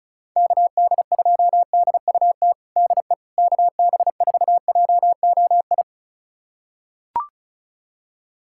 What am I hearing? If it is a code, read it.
Morse code, heard as a keyed tone, about 36 wpm, 690 Hz. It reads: KD2DUT DE KB4JOI
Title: USA Callsign DE Callsign - 6 Characters - 35wpm